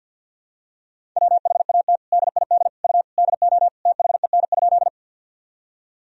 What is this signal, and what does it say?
Morse code, heard as a keyed tone, about 46 wpm, 700 Hz. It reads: WHAT DID U DO THEN?
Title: Sentences from Top 100 Words and Abbreviations 50wpm